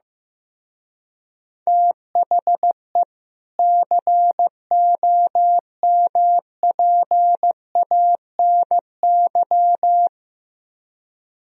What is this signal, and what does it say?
Morse code, heard as a keyed tone, about 15 wpm, 700 Hz. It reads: THE COMPANY
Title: Speed-Racing - Sets of 2 Words 15wpm